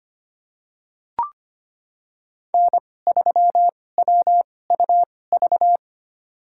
Morse code, heard as a keyed tone, about 25 wpm, spaced as for 18 wpm, 700 Hz. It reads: N3WUV
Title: USA Call Signs - 5 Characters - 2x Character Spacing - 25wpm